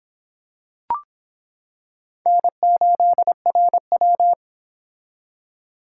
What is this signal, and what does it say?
Morse code, heard as a keyed tone, about 25 wpm, 700 Hz. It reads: N8RW